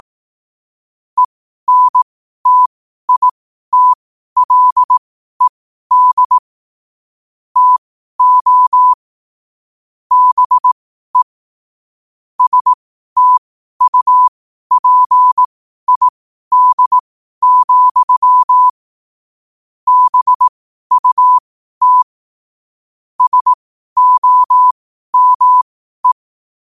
Morse code, heard as a keyed tone, about 18 wpm, spaced as for 12 wpm, 1000 Hz. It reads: ENTITLED TO BE STUPID, BUT SOME
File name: qotdfiles / 12wpm